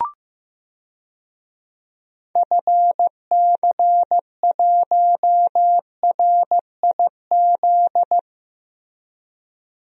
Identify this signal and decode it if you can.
Morse code, heard as a keyed tone, about 16 wpm, 700 Hz. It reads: FC1RIZ
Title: DXCC-100 Call Signs - 6 Characters - Repeated 2x - 15wpm